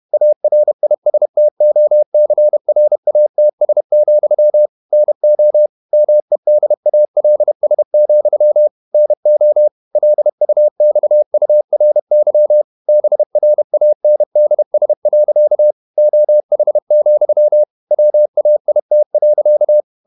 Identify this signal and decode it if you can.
Morse code, heard as a keyed tone, about 31 wpm, 600 Hz. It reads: ARISTOCRATS, NO MEDALS, NO LUXURY BRANDS. OH, WAIT.
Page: Never 31wpm | CW med Gnister